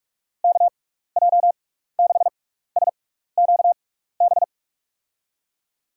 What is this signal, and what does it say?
Morse code, heard as a keyed tone, about 45 wpm, spaced as for 15 wpm, 700 Hz. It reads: KJ6SQB
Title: USA Call Signs - 6 Characters - 6x Character Spacing - 45wpm